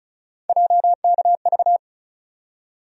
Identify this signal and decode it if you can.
Morse code, heard as a keyed tone, about 36 wpm, 700 Hz. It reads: JKV
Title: USA Call Sign Postfixes - 3 Letters 35wpm